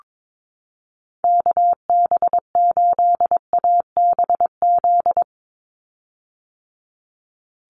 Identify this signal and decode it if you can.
Morse code, heard as a keyed tone, about 22 wpm, 700 Hz. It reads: KB8ABZ
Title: USA Call Signs 22wpm